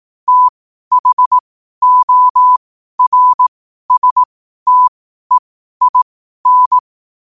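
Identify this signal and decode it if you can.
Morse code, heard as a keyed tone, about 18 wpm, spaced as for 12 wpm, 1000 Hz. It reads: THORSTEIN
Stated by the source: Morse code quotes at 12 words per minute.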